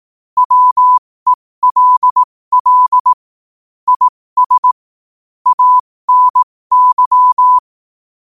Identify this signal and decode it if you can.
Morse code, heard as a keyed tone, about 18 wpm, spaced as for 15 wpm, 1000 Hz. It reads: WELL IS ANY
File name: qotdfiles / 15wpm